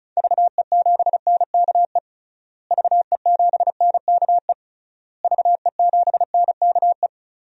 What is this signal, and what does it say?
Morse code, heard as a keyed tone, about 36 wpm, 700 Hz. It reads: VE7NKE VE7NKE VE7NKE
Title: DXCC-100 Call Signs - 6 Characters - Repeated 3x - 35wpm